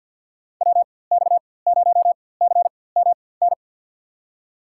Morse code, heard as a keyed tone, about 52 wpm, spaced as for 23 wpm, 700 Hz. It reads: WX0CMN
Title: USA Call Signs - 4x Character Spacing - 50wpm